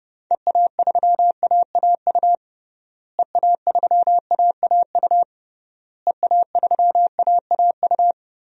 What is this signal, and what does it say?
Morse code, heard as a keyed tone, about 30 wpm, 700 Hz. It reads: EA3AAU EA3AAU EA3AAU
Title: DXCC-100 Call Signs - 6 Characters - Repeated 3x - 30wpm